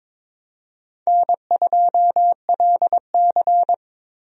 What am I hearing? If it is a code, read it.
Morse code, heard as a keyed tone, about 22 wpm, 700 Hz. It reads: N2LC
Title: USA Call Signs - 4 Characters - Repeated 3x - 22wpm